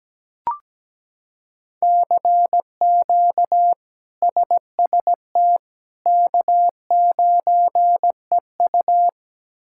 Morse code, heard as a keyed tone, about 17 wpm, 700 Hz. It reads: CQ SST K9EU
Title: K1USN SST QSO Practice - 17wpm